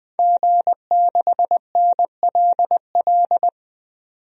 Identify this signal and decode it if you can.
Morse code, heard as a keyed tone, about 20 wpm, 700 Hz. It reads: G6NLL
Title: DXCC-100 Call Signs - 5 Characters - 20wpm